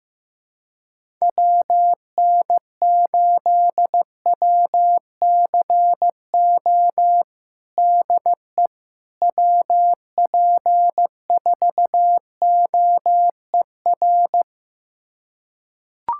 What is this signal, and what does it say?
Morse code, heard as a keyed tone, about 15 wpm, 700 Hz. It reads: WN8WCO DE WP4OER
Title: USA Callsign DE Callsign - 6 Characters - 15wpm